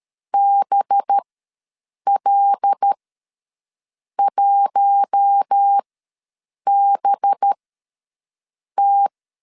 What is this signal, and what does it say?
Morse code, heard as a keyed tone, about 13 wpm, spaced as for 6 wpm, 800 Hz. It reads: BL1BT
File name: LEZIONI / 5 LBVC / 5 LB ws 12wpm 3034